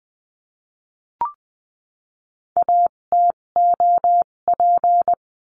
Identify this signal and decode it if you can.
Morse code, heard as a keyed tone, about 20 wpm, spaced as for 17 wpm, 700 Hz. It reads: ATOP
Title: 257 Lesson 42b - Words 14-20wpm